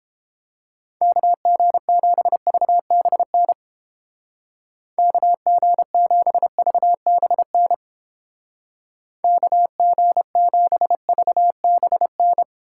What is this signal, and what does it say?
Morse code, heard as a keyed tone, about 30 wpm, 700 Hz. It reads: KG7VBN KG7VBN KG7VBN